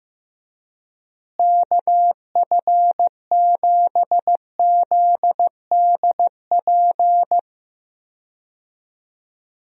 Morse code, heard as a keyed tone, about 15 wpm, 700 Hz. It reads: KF7ZDP
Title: USA Call Signs - 6 Characters - Repeated 3x - 15wpm